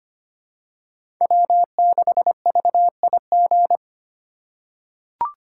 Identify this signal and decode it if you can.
Morse code, heard as a keyed tone, about 25 wpm, 700 Hz. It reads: W6VIG
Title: USA Call Signs - 5 Characters - 8x Character Spacing - 25wpm